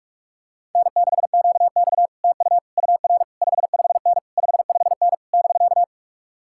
Morse code, heard as a keyed tone, about 50 wpm, 700 Hz. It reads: N6QX TU UR 55N 55N <BK>